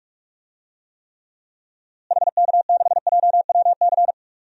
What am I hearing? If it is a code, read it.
Morse code, heard as a keyed tone, about 46 wpm, 700 Hz. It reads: HK6JWC
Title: DXCC-100 Call Signs - 6 Characters - Repeated 3x - 45wpm